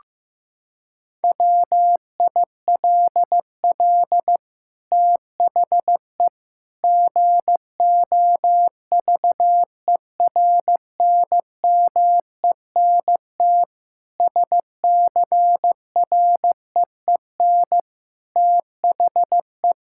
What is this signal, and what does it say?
Morse code, heard as a keyed tone, about 15 wpm, 700 Hz. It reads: WILL THE GOVERNMENT SCREEN THE
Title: Sentences from Top 1000 Words - Review Part 6 - 15wpm